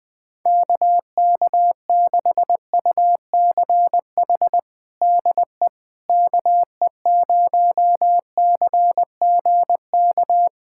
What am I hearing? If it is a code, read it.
Morse code, heard as a keyed tone, about 20 wpm, 700 Hz. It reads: KK6UCH DE KE0CGK
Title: USA Callsign DE Callsign 20wpm